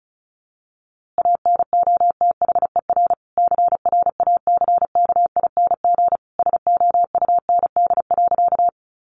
Morse code, heard as a keyed tone, about 34 wpm, 700 Hz. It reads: ANOTHER CRACKING SOUND.
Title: A Sound Of Thunder 35wpm